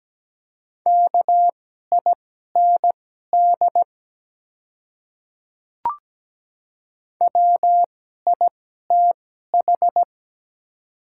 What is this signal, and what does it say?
Morse code, heard as a keyed tone, about 17 wpm, spaced as for 12 wpm, 700 Hz. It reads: KIND WITH
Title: 2x Character Spacing - Top 200 Words - 17wpm